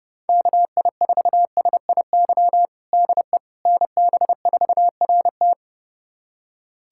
Morse code, heard as a keyed tone, about 30 wpm, 700 Hz. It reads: KI4SIY DE NB4RT
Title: USA Callsign DE Callsign 30wpm